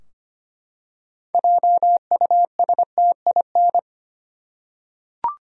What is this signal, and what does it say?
Morse code, heard as a keyed tone, about 25 wpm, 700 Hz. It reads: JUSTIN
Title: USA Common Names - 25wpm